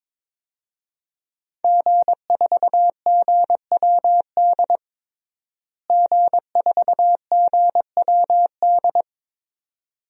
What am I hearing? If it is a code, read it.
Morse code, heard as a keyed tone, about 21 wpm, 700 Hz. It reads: G4GWD G4GWD
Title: DXCC-100 Call Signs - 5 Characters - Repeated 2x - 22wpm